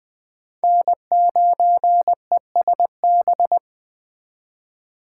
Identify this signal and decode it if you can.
Morse code, heard as a keyed tone, about 20 wpm, 700 Hz. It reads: N9ESB
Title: USA Call Signs - 5 Characters - 8x Character Spacing - 20wpm